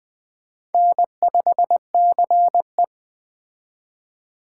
Morse code, heard as a keyed tone, about 21 wpm, 700 Hz. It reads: N5CE
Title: Speed Racing - USA 1x2 Call Signs 17wpm